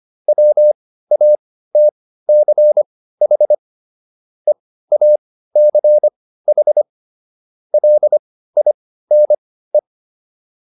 Morse code, heard as a keyed tone, about 25 wpm, spaced as for 15 wpm, 600 Hz. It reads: WATCH EACH LINE